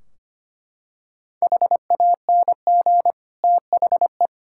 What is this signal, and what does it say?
Morse code, heard as a keyed tone, about 25 wpm, 700 Hz. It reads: HANG THE
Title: Sentences from Top 1600 Words 25wpm